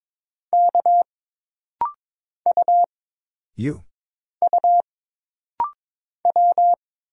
Morse code, heard as a keyed tone, about 22 wpm, 700 Hz. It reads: K U U W
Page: This rapid-fire timing is geared to push your ICR (Instant Character Recognition) skills.